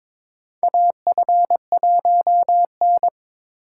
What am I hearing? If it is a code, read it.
Morse code, heard as a keyed tone, about 22 wpm, 700 Hz. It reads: AF1N
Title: USA Call Signs - 4 Characters - 4x Character Spacing - 22wpm